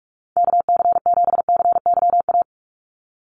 Morse code, heard as a keyed tone, about 44 wpm, 700 Hz. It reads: KC7CYA
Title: USA Call Signs 45wpm